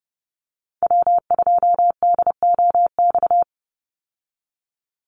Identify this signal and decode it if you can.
Morse code, heard as a keyed tone, about 30 wpm, 700 Hz. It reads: W2DOX
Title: USA Call Signs 30wpm